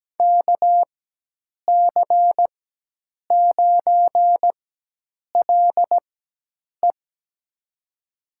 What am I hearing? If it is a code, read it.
Morse code, heard as a keyed tone, about 17 wpm, spaced as for 8 wpm, 700 Hz. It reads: KC9LE
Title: USA Call Signs - 5 Characters - 4x Character Spacing - 17wpm